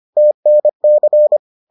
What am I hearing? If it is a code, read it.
Morse code, heard as a keyed tone, about 25 wpm, 600 Hz. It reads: TNC